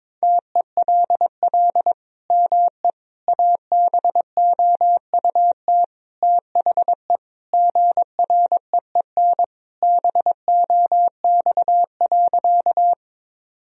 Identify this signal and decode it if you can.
Morse code, heard as a keyed tone, about 22 wpm, 700 Hz. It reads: TELL ME ABOUT THE GREEN BOX.
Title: Sentences from Top 1000 Words - Review Part 2 - 22wpm